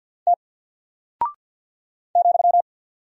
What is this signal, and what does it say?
Morse code, heard as a keyed tone, about 48 wpm, 700 Hz. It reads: T ,
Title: Single Character (Letter:Number:Punctuation:Prosign) 50wpm